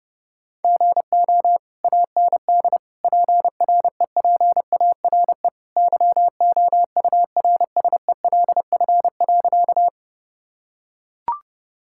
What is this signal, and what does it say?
Morse code, heard as a keyed tone, about 30 wpm, 700 Hz. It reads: GO AND PREPARE YOURSELF.
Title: Sentences from Top 1000 Words - Review Part 6 - 30wpm